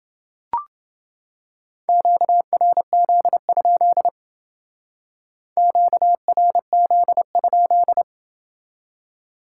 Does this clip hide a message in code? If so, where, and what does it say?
Yes: Morse code from 1.9 to 8.0 s, heard as a keyed tone, about 29 wpm, 700 Hz. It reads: QRZ? QRZ?